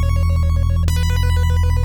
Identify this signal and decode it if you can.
There is no code.